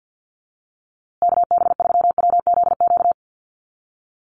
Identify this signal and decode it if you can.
Morse code, heard as a keyed tone, about 50 wpm, 700 Hz. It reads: KB3WZQ
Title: USA Call Signs Repeated 3x 50wpm